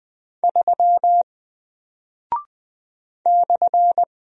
Morse code, heard as a keyed tone, about 20 wpm, 700 Hz. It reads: 3 /
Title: Single Character (Letter:Number:Punctuation:Prosign) 20wpm